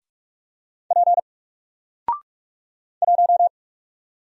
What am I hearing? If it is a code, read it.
Morse code, heard as a keyed tone, about 46 wpm, 700 Hz. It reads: P 1
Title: Single Character (Letter:Number:Punctuation:Prosign) Alt 45wpm